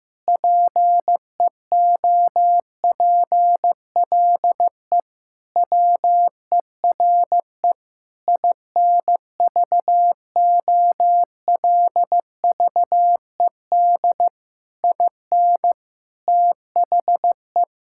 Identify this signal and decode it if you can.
Morse code, heard as a keyed tone, about 16 wpm, 700 Hz. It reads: PEOPLE WERE INVOLVED IN THE
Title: Sentences from Top 1000 Words - Review Part 5 - 15wpm